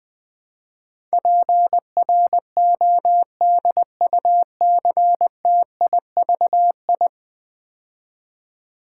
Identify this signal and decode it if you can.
Morse code, heard as a keyed tone, about 20 wpm, 700 Hz. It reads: PRODUCTIVI
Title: Build A Word - 5 Letters At A Time - 12-Letter Words - 20wpm